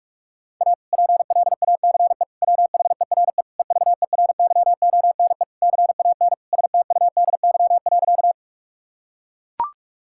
Morse code, heard as a keyed tone, about 41 wpm, 700 Hz. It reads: A PLACE WHERE EVERYONE CAN STUDY.
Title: Sentences from Top 1000 Words - Review Part 5 - 45wpm